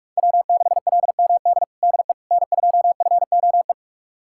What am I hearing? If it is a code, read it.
Morse code, heard as a keyed tone, about 46 wpm, 700 Hz. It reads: W6LMD DE N2FOE
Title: USA Callsign DE Callsign - 5 Characters - 45wpm